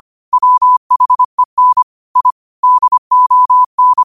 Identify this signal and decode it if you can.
Morse code, heard as a keyed tone, about 25 wpm, 1000 Hz. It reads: WHEN I DON
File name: 25wpm